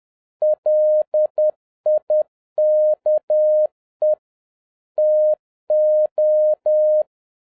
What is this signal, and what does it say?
Morse code, heard as a keyed tone, about 10 wpm, 610 Hz. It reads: LIKE TO